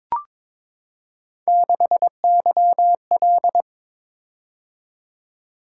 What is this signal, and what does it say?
Morse code, heard as a keyed tone, about 22 wpm, 700 Hz. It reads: 6YL
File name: practice / 22wpm / Three Letter-Number Combinations - 22wpm